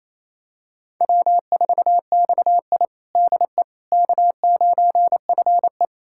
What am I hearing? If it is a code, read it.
Morse code, heard as a keyed tone, about 28 wpm, 700 Hz. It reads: W4XI DE K9FE
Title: USA Callsign DE Callsign - 4 Characters - 28wpm